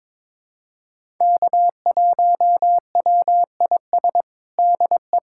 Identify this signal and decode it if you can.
Morse code, heard as a keyed tone, about 22 wpm, 700 Hz. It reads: K1WIS DE
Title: USA Callsign DE Callsign 22wpm